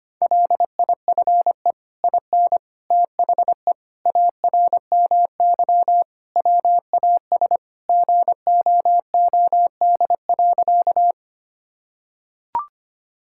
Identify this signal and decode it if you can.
Morse code, heard as a keyed tone, about 25 wpm, 700 Hz. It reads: LIFE IN THE ARMY WAS GOOD.
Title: Sentences from Top 1000 Words - Review Part 5 - 25wpm